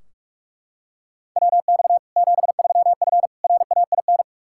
Morse code, heard as a keyed tone, about 41 wpm, 700 Hz. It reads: WX 73F RAIN
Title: Practice QSO - Exchange 2 - Info - 45wpm